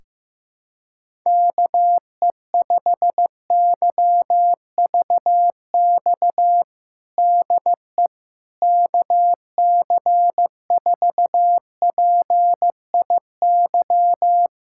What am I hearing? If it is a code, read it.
Morse code, heard as a keyed tone, about 15 wpm, 700 Hz. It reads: KE5YVX DE KC4PIY
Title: USA Callsign DE Callsign - 6 Characters - 15wpm